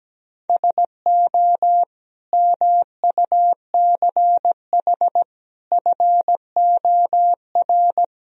Morse code, heard as a keyed tone, about 17 wpm, 700 Hz. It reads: SO MUCH FOR
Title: Speed-Racing - Sets of 5 Words 17wpm